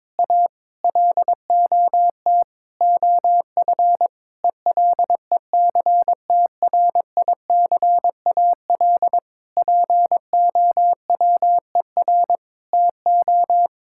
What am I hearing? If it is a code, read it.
Morse code, heard as a keyed tone, about 22 wpm, 700 Hz. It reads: A LOT OF ELECTRICAL POWER TO
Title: Sentences from Top 1000 Words - Review Part 6 - 22wpm